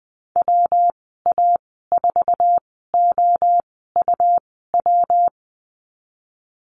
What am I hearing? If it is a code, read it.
Morse code, heard as a keyed tone, about 20 wpm, spaced as for 15 wpm, 700 Hz. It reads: WA4OUW
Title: 167 Lesson 27g - Number 7 - Call Signs 20wpm